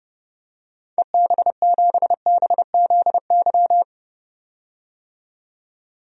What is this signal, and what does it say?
Morse code, heard as a keyed tone, about 32 wpm, 700 Hz. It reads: EB7BZY
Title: DXCC-100 Call Signs - 6 Characters - 30wpm